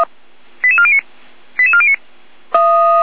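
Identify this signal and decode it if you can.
Five-tone selective call (ZVEI), heard as quick runs of five tones, tones of about 70 ms. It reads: 88409 88409
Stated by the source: ZVEI five-tone selective call received on an SDR.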